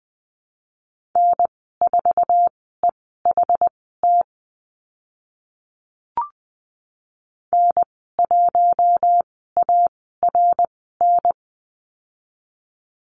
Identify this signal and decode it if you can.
Morse code, heard as a keyed tone, about 20 wpm, spaced as for 14 wpm, 700 Hz. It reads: N4EHT N1ARN
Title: 062 Lesson 11g - Letter h - Call Signs 20wpm